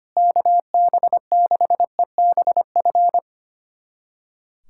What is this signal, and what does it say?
Morse code, heard as a keyed tone, about 25 wpm, 700 Hz. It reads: KB6EBF
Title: USA Callsign DE Callsign 25wpm